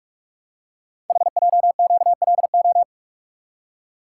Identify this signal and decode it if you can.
Morse code, heard as a keyed tone, about 44 wpm, 700 Hz. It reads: HJQLO
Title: Code Groups - 5 Letters - 45wpm